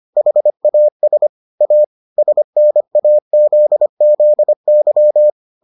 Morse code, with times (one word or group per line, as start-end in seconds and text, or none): 0.2-1.3 HAS
1.6-1.8 A
2.2-5.3 SNAZZY